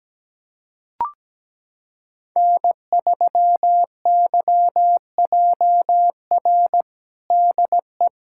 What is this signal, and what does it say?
Morse code, heard as a keyed tone, about 17 wpm, 700 Hz. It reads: N3YJR DE
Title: USA Callsign DE Callsign - 5 Characters - 17wpm